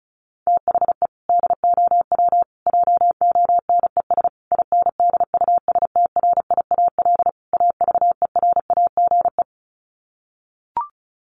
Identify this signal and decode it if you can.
Morse code, heard as a keyed tone, about 34 wpm, 700 Hz. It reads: THE DOW JONES INDUSTRIAL AVERAGE
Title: Sets of 5 Words 35wpm